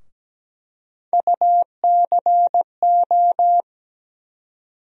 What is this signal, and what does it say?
Morse code, heard as a keyed tone, about 17 wpm, 700 Hz. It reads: UCO
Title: USA Call Sign Postfixes - 3 Letters 17wpm